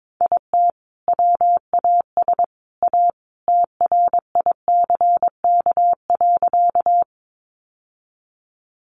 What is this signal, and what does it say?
Morse code, heard as a keyed tone, about 22 wpm, 700 Hz. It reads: IT WAS A TRICK.
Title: Sentences from Top 1500 Words 22wpm